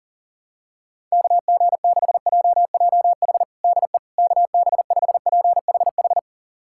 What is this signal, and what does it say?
Morse code, heard as a keyed tone, about 40 wpm, 700 Hz. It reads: KG6JJH DE KB5PHH
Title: USA Callsign DE Callsign 40wpm